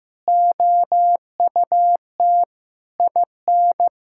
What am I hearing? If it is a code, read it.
Morse code, heard as a keyed tone, about 15 wpm, 700 Hz. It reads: OUT IN
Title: Sentences from Top 1000 Words - Review Part 5 - 15wpm